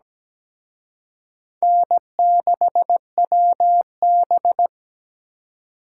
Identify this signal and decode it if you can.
Morse code, heard as a keyed tone, about 20 wpm, 700 Hz. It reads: N6WB
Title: Speed Racing - USA 1x2 Call Signs 17wpm